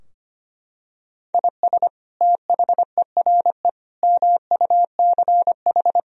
Morse code, heard as a keyed tone, about 25 wpm, 700 Hz. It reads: IS THERE MUCH
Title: Sentences from Top 1000 Words - Review Part 4 - 25wpm